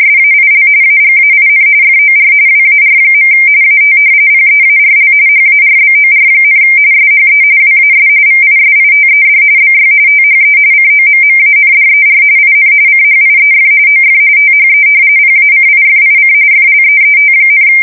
RTTY, about 45.45 baud. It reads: SAIL AGAIN FOR WAR. AT PEARL HARBOR, SHE WAS VISITED BY THE COMMANDER IN CHIEF OF THE PACIFIC FLEET,
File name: fsk170 / battleshipnj